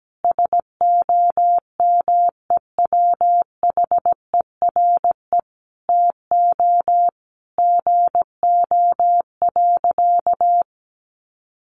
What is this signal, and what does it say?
Morse code, heard as a keyed tone, about 17 wpm, 700 Hz. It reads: SOMEWHERE TO GO.
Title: Sentences from Top 1400 Words 17wpm